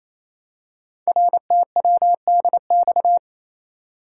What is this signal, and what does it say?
Morse code, heard as a keyed tone, about 27 wpm, 700 Hz. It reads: RTWDX
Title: Code Groups - 5 Letters - 28wpm